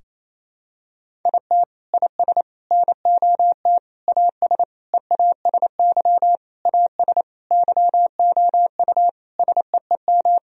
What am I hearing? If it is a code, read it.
Morse code, heard as a keyed tone, about 28 wpm, 700 Hz. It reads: IT IS NOT AS EASY AS YOU SEEM
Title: Sentences from Top 1000 Words - Review Part 1 - 28wpm